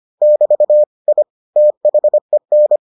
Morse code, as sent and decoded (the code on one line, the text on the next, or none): -...- / .. / - .... -.
<BT> I THEN